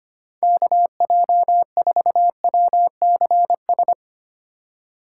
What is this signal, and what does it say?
Morse code, heard as a keyed tone, about 33 wpm, 700 Hz. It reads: KJ4WCS